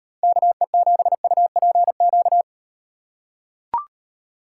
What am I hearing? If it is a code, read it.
Morse code, heard as a keyed tone, about 38 wpm, 700 Hz. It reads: KE7UPQ